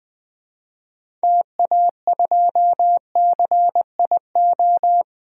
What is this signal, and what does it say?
Morse code, heard as a keyed tone, about 20 wpm, 700 Hz. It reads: TA2CIO
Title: DXCC-100 Call Signs - 6 Characters - 20wpm